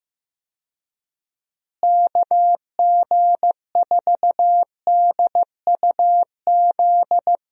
Morse code, heard as a keyed tone, about 15 wpm, 700 Hz. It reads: KG4DUZ
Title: USA Call Signs - 6 Characters - Repeated 3x - 15wpm